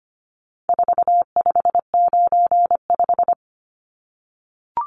Morse code, heard as a keyed tone, about 25 wpm, 700 Hz. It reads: 4595
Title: Numbers - 4-Digit 25wpm